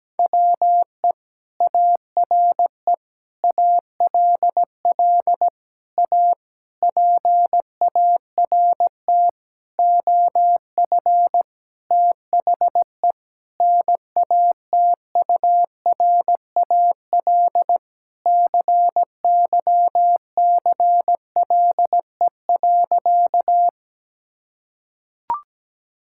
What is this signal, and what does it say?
Morse code, heard as a keyed tone, about 17 wpm, 700 Hz. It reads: WE ARE ALL A PART OF THE NATURAL CYCLE.
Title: Sentences from Top 1000 Words - Review Part 6 - 17wpm